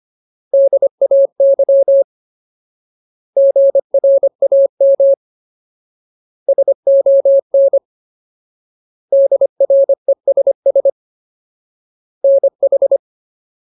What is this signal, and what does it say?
Morse code, heard as a keyed tone, about 25 wpm, 550 Hz. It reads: DAY GRAM SON DRESS NH
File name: Ultimate Word List / 25WPM Ultimate